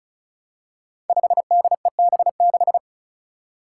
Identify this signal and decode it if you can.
Morse code, heard as a keyed tone, about 36 wpm, 700 Hz. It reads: 5DEB6
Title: Five Challenging Letter-Number Combinations 35wpm